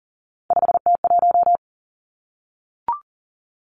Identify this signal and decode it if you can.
Morse code, heard as a keyed tone, about 38 wpm, 700 Hz. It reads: 5T1
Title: Three Challenging Letter-Number Combinations 40wpm